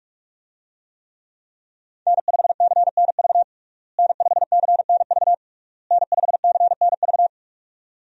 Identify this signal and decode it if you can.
Morse code, heard as a keyed tone, about 42 wpm, 700 Hz. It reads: N5CNV N5CNV N5CNV
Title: USA Call Signs - 5 Characters - Repeated 3x - 45wpm